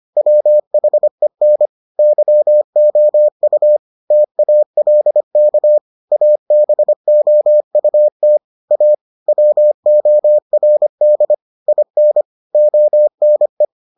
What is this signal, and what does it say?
Morse code, heard as a keyed tone, about 25 wpm, 600 Hz. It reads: WHEN YOU TALK ABOUT A WORD IN ONE